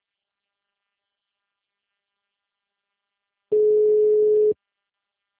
Ringback tone (Europe, 425 Hz).